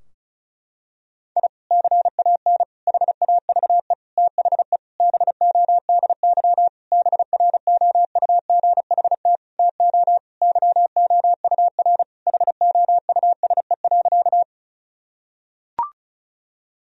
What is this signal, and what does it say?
Morse code, heard as a keyed tone, about 34 wpm, 700 Hz. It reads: I CAN HAVE THE BODY BROUGHT TO YOUR HOUSE.
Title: Sentences from Top 1000 Words - Review Part 2 - 35wpm